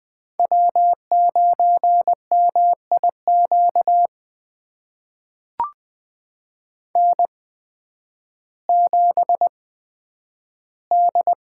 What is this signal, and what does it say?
Morse code, heard as a keyed tone, about 20 wpm, spaced as for 11 wpm, 700 Hz. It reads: W9MIQ N7D
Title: USA Call Signs - 5 Characters - 8x Character Spacing - 20wpm